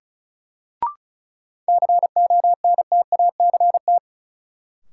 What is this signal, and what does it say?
Morse code, heard as a keyed tone, about 34 wpm, 700 Hz. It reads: CONTACT
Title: Sentences from Top 1000 Words - Review Part 7 - New Word Drill - 35wpm